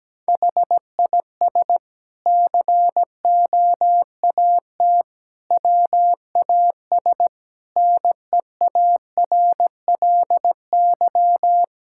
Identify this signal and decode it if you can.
Morse code, heard as a keyed tone, about 17 wpm, 700 Hz. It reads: HIS COAT WAS NEARLY
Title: Sentences from Top 1600 Words 17wpm